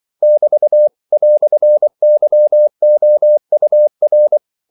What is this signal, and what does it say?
Morse code, heard as a keyed tone, about 24 wpm, 600 Hz. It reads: <BT> "YOUR
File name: CW / Practice / EnglishQuotations0424